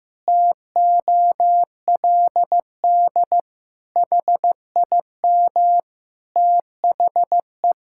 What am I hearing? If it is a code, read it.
Morse code, heard as a keyed tone, about 15 wpm, 700 Hz. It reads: TOLD HIM THE
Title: Pangram Challenge v2 - 15wpm